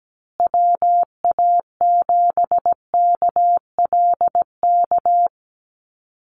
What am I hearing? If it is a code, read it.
Morse code, heard as a keyed tone, about 17 wpm, 700 Hz. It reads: WA7KLK
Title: USA Call Signs Repeated 3x 17wpm